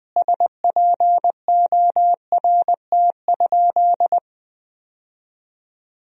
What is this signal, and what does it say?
Morse code, heard as a keyed tone, about 20 wpm, 700 Hz. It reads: SPORT?
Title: Sentences from Top 1000 Words - Review Part 4 - 20wpm